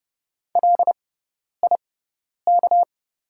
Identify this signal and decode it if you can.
Morse code, heard as a keyed tone, about 30 wpm, spaced as for 10 wpm, 700 Hz. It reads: LIK
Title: 6x Character Spacing - Top 100 Words - 30wpm